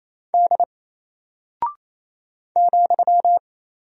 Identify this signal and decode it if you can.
Morse code, heard as a keyed tone, about 28 wpm, 700 Hz. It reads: D ,
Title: Single Character (Letter:Number:Punctuation:Prosign) Alt 28wpm